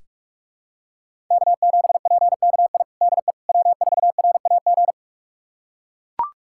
Morse code, heard as a keyed tone, about 41 wpm, 700 Hz. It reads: K7PKI DE W4RAG
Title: USA Callsign DE Callsign - 5 Characters - 45wpm